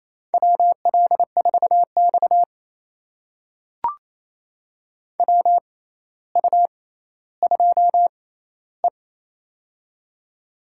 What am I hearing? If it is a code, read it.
Morse code, heard as a keyed tone, about 28 wpm, spaced as for 14 wpm, 700 Hz. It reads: WL4X WU2E
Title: USA Call Signs - 4 Characters - 6x Character Spacing - 28wpm